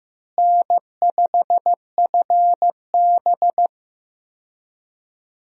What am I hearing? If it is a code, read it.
Morse code, heard as a keyed tone, about 15 wpm, 700 Hz. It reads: N5FB
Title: USA Call Signs - 4 Characters - Repeated 2x - 15wpm